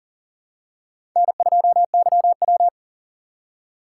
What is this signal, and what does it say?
Morse code, heard as a keyed tone, about 41 wpm, 700 Hz. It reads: N2YW
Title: USA Call Signs - 4 Characters - 2x Character Spacing - 40wpm